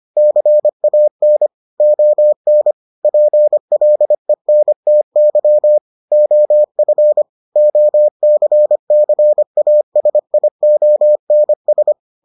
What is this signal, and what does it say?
Morse code, heard as a keyed tone, about 25 wpm, 600 Hz. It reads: CAN ON PLENTY OF OCCASIONS